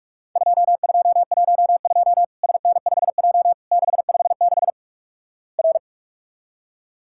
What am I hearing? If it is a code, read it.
Morse code, heard as a keyed tone, about 48 wpm, 700 Hz. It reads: 2212 SN5J 656 R
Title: Worked All Europe DX Contest - QTC Exchange - 45wpm